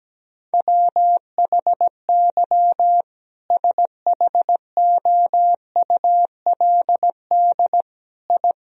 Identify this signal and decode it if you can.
Morse code, heard as a keyed tone, about 17 wpm, 700 Hz. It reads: WHY SHOULD I
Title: Sentences from Top 1000 Words - Review Part 3 - 17wpm